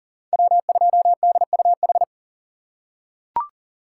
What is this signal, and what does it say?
Morse code, heard as a keyed tone, about 40 wpm, 700 Hz. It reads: W2DUH
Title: USA Call Signs - 5 Characters - 4x Character Spacing - 40wpm